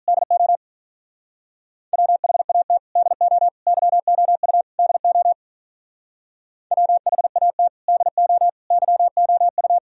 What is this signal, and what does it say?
Morse code, heard as a keyed tone, about 42 wpm, 700 Hz. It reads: DO WHAT DO YOU DO WHAT DO YOU
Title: Speed-Racing - Sets of 4 Words 35wpm